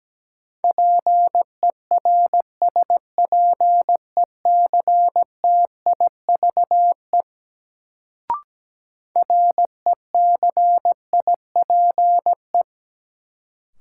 Morse code, heard as a keyed tone, about 17 wpm, 700 Hz. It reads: PERSPECTIVE RECIPE
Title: Sentences from Top 1200 Words - New Word Drill Alt - 17wpm